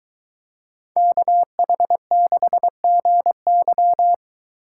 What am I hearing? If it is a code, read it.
Morse code, heard as a keyed tone, about 22 wpm, 700 Hz. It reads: KH6GY